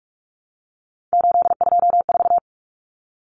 Morse code, heard as a keyed tone, about 44 wpm, 700 Hz. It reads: 824
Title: Common Cut Numbers - 3-Digit 45wpm